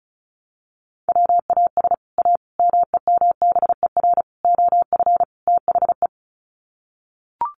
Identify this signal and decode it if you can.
Morse code, heard as a keyed tone, about 34 wpm, 700 Hz. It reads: WAS A MEMBER OF THE
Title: Sets of 5 Words 35wpm